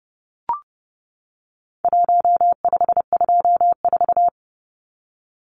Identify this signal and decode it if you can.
Morse code, heard as a keyed tone, about 30 wpm, 700 Hz. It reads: 1524
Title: Numbers - 4-Digit 30wpm